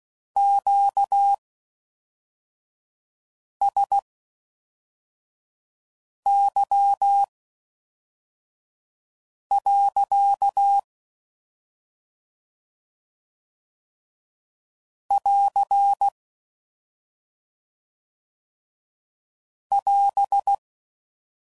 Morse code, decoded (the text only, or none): QSY. <AR> <AS>